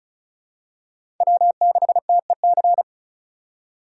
Morse code, heard as a keyed tone, about 36 wpm, 700 Hz. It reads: W6TEC
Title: USA Call Signs - 5 Characters - 3x Character Spacing - 35wpm